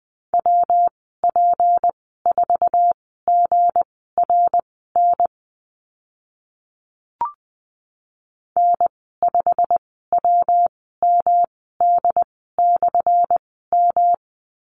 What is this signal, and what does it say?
Morse code, heard as a keyed tone, about 20 wpm, spaced as for 14 wpm, 700 Hz. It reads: WP4GRN N5WMD/M
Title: 179 Lesson 29g - Slash - Call Signs 20wpm